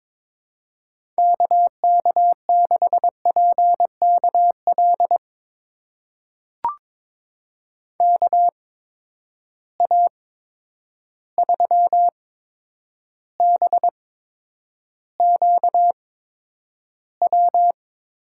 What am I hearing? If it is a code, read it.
Morse code, heard as a keyed tone, about 22 wpm, spaced as for 9 wpm, 700 Hz. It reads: KK6PKL KA3BQW
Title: USA Call Signs - 6 Characters - 8x Character Spacing - 22wpm